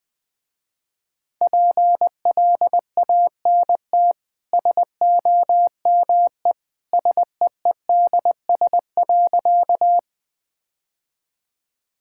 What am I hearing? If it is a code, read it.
Morse code, heard as a keyed tone, about 20 wpm, 700 Hz. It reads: PLANT SOME SEEDS.
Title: Easy Sentences with Hints - 20wpm